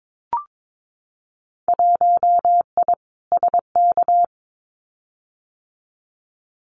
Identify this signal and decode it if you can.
Morse code, heard as a keyed tone, about 22 wpm, 700 Hz. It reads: 1I SK
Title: Winter Field Day Exchanges Encore 22wpm